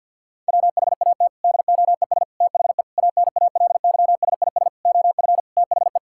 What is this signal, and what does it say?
Morse code, heard as a keyed tone, about 50 wpm, 700 Hz. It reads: WHAT DOES THE ANALYSIS OF THE
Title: Sentences from Top 1000 Words - Review Part 6 - 50wpm